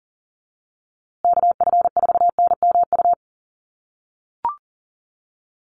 Morse code, heard as a keyed tone, about 40 wpm, 700 Hz. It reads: KF4NMU
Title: USA Call Signs Repeated 3x 40wpm